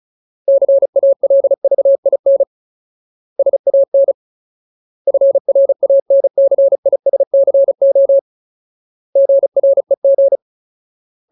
Morse code, heard as a keyed tone, about 34 wpm, 550 Hz. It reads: CALVIN SAN FRANCISCO GREG
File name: Ultimate Word List / 35WPM Ultimate